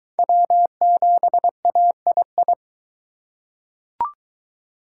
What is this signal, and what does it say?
Morse code, heard as a keyed tone, about 21 wpm, 700 Hz. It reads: W7AII